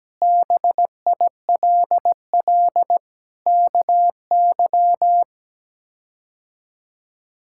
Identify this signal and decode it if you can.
Morse code, heard as a keyed tone, about 17 wpm, 700 Hz. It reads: BILL KY
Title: K1USN SST QSO Practice - 17wpm